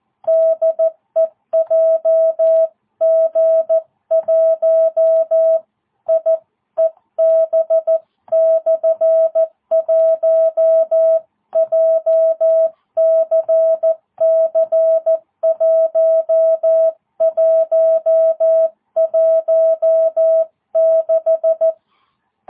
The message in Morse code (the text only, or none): DE JG1IEB/1 JCC 1116